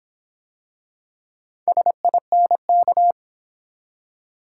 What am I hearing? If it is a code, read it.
Morse code, heard as a keyed tone, about 26 wpm, 700 Hz. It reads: SINK